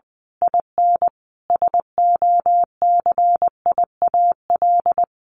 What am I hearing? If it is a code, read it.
Morse code, heard as a keyed tone, about 20 wpm, 700 Hz. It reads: IN SOCIAL
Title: A Sound Of Thunder 20wpm